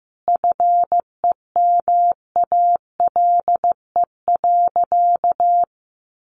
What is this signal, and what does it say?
Morse code, heard as a keyed tone, about 15 wpm, 700 Hz. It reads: FEMALE.
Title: Easy Phrases 15wpm